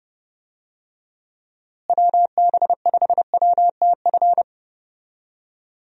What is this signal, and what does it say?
Morse code, heard as a keyed tone, about 32 wpm, 700 Hz. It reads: WB5WTF
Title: USA Call Signs - 6 Characters - Repeated 2x - 30wpm